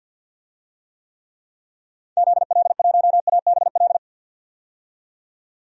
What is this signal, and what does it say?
Morse code, heard as a keyed tone, about 50 wpm, 700 Hz. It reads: ZL1ABL
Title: DXCC-100 Call Signs - 6 Characters - 50wpm